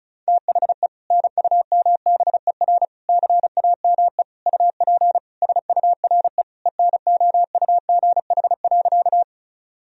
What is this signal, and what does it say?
Morse code, heard as a keyed tone, about 34 wpm, 700 Hz. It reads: THE NUMBER CAME UP SURE ENOUGH.
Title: Sentences from Top 1000 Words - Review Part 1 - 35wpm